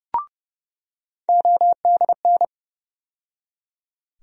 Morse code, heard as a keyed tone, about 30 wpm, 700 Hz. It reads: ODN
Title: USA Call Sign Postfixes - 3 Letters 30wpm